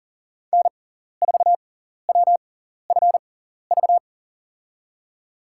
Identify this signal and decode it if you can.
Morse code, heard as a keyed tone, about 40 wpm, spaced as for 14 wpm, 700 Hz. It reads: N4WFV
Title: USA Call Signs - 5 Characters - 6x Character Spacing - 40wpm